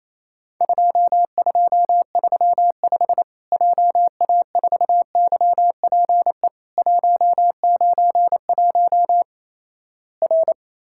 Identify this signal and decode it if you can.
Morse code, heard as a keyed tone, about 28 wpm, 700 Hz. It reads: 2235 JA4YPE 191 R
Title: Worked All Europe DX Contest - QTC Exchange - 28wpm